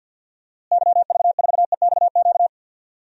Morse code, heard as a keyed tone, about 47 wpm, 700 Hz. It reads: YV4EXQ
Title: DXCC-100 Call Signs - 6 Characters - Repeated 2x - 50wpm